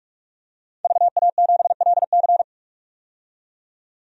Morse code, heard as a keyed tone, about 44 wpm, 700 Hz. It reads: VA7LC
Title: DXCC-100 Call Signs - 5 Characters - 45wpm